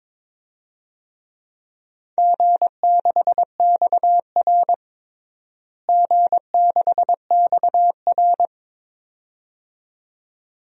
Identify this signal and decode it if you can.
Morse code, heard as a keyed tone, about 22 wpm, 700 Hz. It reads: G6XR G6XR
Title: DXCC-100 Call Signs - 4 Characters - Repeated 2x - 22wpm